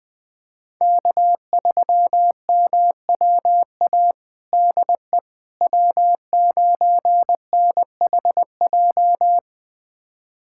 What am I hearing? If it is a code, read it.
Morse code, heard as a keyed tone, about 20 wpm, 700 Hz. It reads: K3MWA DE W9NHJ
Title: USA Callsign DE Callsign - 5 Characters - 20wpm